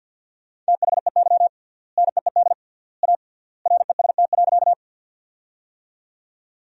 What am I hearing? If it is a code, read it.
Morse code, heard as a keyed tone, about 54 wpm, 700 Hz. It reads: THEY NEED A REST.
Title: 3x Word-Spacing - Sentences from Top 500 Words - 50wpm